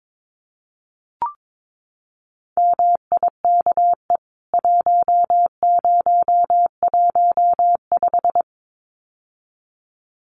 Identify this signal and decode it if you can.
Morse code, heard as a keyed tone, about 22 wpm, 700 Hz. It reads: MIKE 1015
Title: CWT Exchange 22wpm